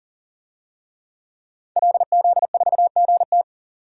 Morse code, heard as a keyed tone, about 40 wpm, 700 Hz. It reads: LZ4GT
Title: DXCC-100 Call Signs - 5 Characters - 40wpm